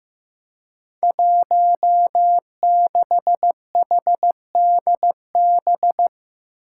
Morse code, heard as a keyed tone, about 15 wpm, 700 Hz. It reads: 16HDB
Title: Five Challenging Letter-Number Combinations 15wpm